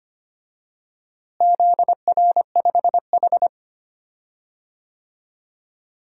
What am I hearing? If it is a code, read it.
Morse code, heard as a keyed tone, about 25 wpm, 700 Hz. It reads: ZR5H
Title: DXCC-100 Call Signs - 4 Characters - Repeated 2x - 25wpm